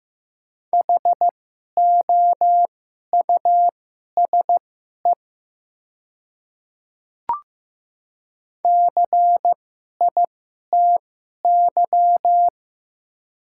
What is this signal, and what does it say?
Morse code, heard as a keyed tone, about 15 wpm, spaced as for 11 wpm, 700 Hz. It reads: HOUSE CITY
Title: 2x Character Spacing - Top 300 Words - 15wpm